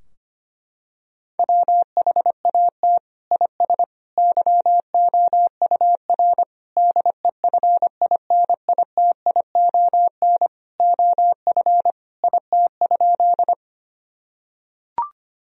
Morse code, heard as a keyed tone, about 25 wpm, 700 Hz. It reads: WHAT IS YOUR DEFINITION OF IT?
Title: Sentences from Top 1000 Words - Review Part 4 - 25wpm